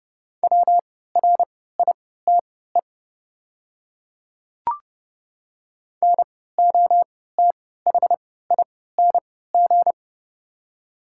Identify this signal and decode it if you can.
Morse code, heard as a keyed tone, about 29 wpm, spaced as for 17 wpm, 700 Hz. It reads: WRITE NOTHING
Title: 3x Character Spacing - Top 500 Words - 30wpm